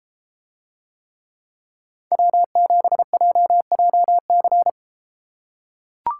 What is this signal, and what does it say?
Morse code, heard as a keyed tone, about 33 wpm, 700 Hz. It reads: W7JJC